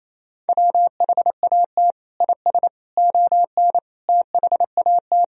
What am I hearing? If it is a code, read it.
Morse code, heard as a keyed tone, about 28 wpm, 700 Hz. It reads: WHAT IS ON THAT
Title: Sentences from Top 1000 Words - Review Part 4 - 28wpm